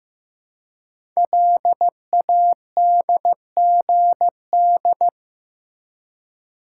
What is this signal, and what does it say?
Morse code, heard as a keyed tone, about 15 wpm, 700 Hz. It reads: LADGD
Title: Code Groups - 5 Letters - 15wpm